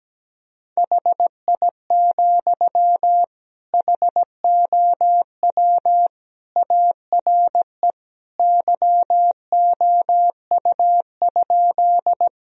Morse code, heard as a keyed tone, about 17 wpm, 700 Hz. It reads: HI, HOW ARE YOU?
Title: Sentences from Top 1000 Words - Review Part 3 - 17wpm